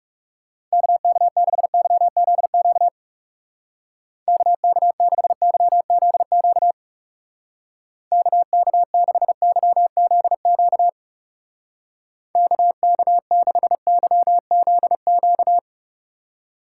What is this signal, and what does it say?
Morse code, heard as a keyed tone, about 36 wpm, 700 Hz. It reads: KK6YZQ KK6YZQ KK6YZQ KK6YZQ
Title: Speed Racing - USA 2x3 Call Signs 30wpm